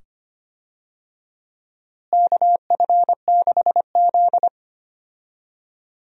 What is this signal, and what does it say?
Morse code, heard as a keyed tone, about 25 wpm, 700 Hz. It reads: KF6Z
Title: USA Call Signs - 4 Characters - Repeated 2x - 25wpm